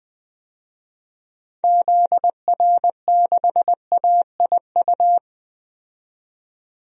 Morse code, heard as a keyed tone, about 20 wpm, 700 Hz. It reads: ZR6AIU
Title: DXCC-100 Call Signs - 6 Characters - 20wpm